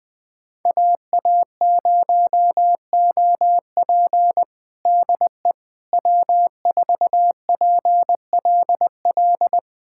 Morse code, heard as a keyed tone, about 20 wpm, 700 Hz. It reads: AA0OP DE W4PLL
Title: USA Callsign DE Callsign - 5 Characters - 20wpm